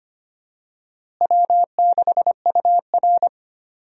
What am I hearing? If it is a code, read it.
Morse code, heard as a keyed tone, about 25 wpm, 700 Hz. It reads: W6UR
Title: USA Call Signs - 4 Characters - Repeated 3x - 25wpm